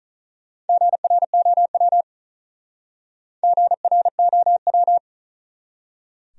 Morse code, heard as a keyed tone, about 46 wpm, 700 Hz. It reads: GROW GROW